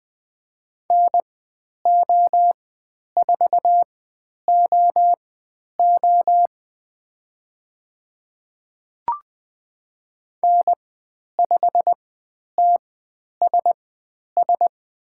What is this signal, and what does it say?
Morse code, heard as a keyed tone, about 20 wpm, spaced as for 10 wpm, 700 Hz. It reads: NO4OO N5TSS
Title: 095 Lesson 16g - Letter c - Call Signs 20wpm